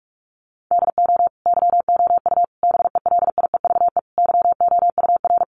Morse code, heard as a keyed tone, about 44 wpm, 700 Hz. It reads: DO YOU BELIEVE YOUR
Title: Sentences from Top 1000 Words 45wpm